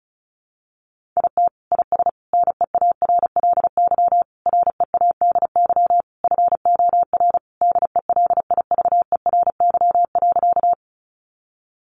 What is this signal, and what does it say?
Morse code, heard as a keyed tone, about 34 wpm, 700 Hz. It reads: IT IS NEARLY READY FOR DELIVERY.
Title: Sentences from Top 1500 Words 35wpm